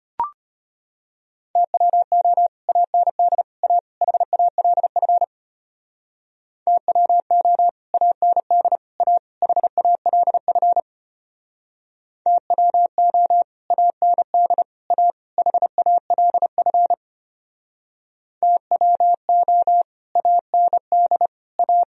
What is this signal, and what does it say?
Morse code, heard as a keyed tone, about 32 wpm, 700 Hz. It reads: TWO AND A HALF TWO AND A HALF TWO AND A HALF TWO AND A